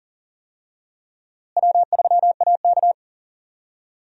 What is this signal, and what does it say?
Morse code, heard as a keyed tone, about 40 wpm, 700 Hz. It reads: W3AK
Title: USA Call Signs - 4 Characters - Repeated 2x - 40wpm